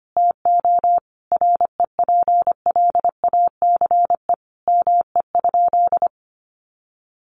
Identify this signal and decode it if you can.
Morse code, heard as a keyed tone, about 25 wpm, 700 Hz. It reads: TO REPLACE ME?
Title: Sentences from Top 1400 Words 25wpm